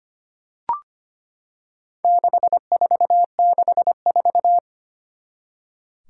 Morse code, heard as a keyed tone, about 25 wpm, 700 Hz. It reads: 6464
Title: Four Challenging Letter-Number Combinations 25wpm